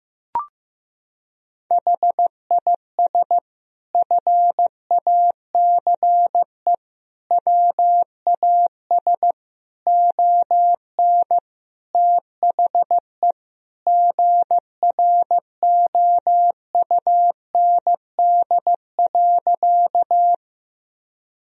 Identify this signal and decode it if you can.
Morse code, heard as a keyed tone, about 15 wpm, 700 Hz. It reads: HIS FACE WAS ON THE GROUND.
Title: Sentences from Top 1000 Words - Review Part 2 - 15wpm